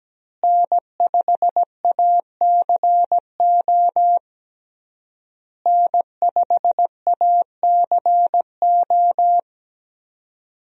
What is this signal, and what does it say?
Morse code, heard as a keyed tone, about 17 wpm, 700 Hz. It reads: N5ACO N5ACO
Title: USA Call Signs - 5 Characters - Repeated 3x - 17wpm